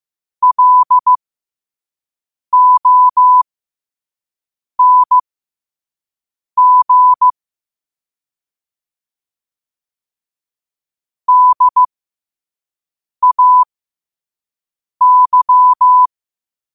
Morse code, heard as a keyed tone, about 15 wpm, spaced as for 5 wpm, 1000 Hz. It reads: LONG DAY
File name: qotdfiles / 5wpm